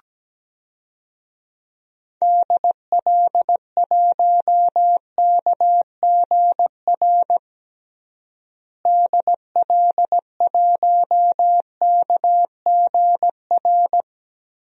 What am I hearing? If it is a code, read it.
Morse code, heard as a keyed tone, about 17 wpm, 700 Hz. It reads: DL1KGR DL1KGR
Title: DXCC-100 Call Signs - 6 Characters - Repeated 3x - 17wpm